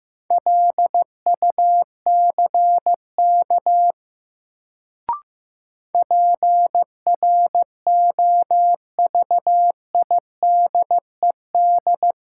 Morse code, heard as a keyed tone, about 15 wpm, 700 Hz. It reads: LUCK PROVIDED
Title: Sentences from Top 1500 Words - New Word Drill Alt - 15wpm